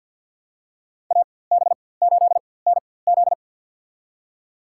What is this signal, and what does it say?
Morse code, heard as a keyed tone, about 53 wpm, spaced as for 24 wpm, 700 Hz. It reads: AB8NZ
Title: USA Call Signs - 5 Characters - 4x Character Spacing - 50wpm